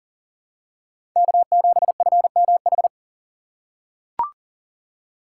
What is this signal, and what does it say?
Morse code, heard as a keyed tone, about 40 wpm, 700 Hz. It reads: K7FMH
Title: USA Call Signs - 2x Character Spacing - 40wpm